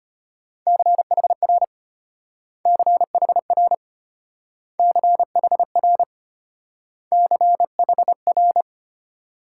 Morse code, heard as a keyed tone, about 31 wpm, 700 Hz. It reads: CHR CHR CHR CHR
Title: Speed-Racing Multi-letter Phonemes - 25wpm